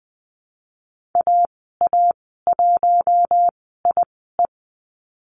Morse code, heard as a keyed tone, about 20 wpm, spaced as for 15 wpm, 700 Hz. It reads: AA1IE
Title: 076 Lesson 13g - Letter L - Call Signs 20wpm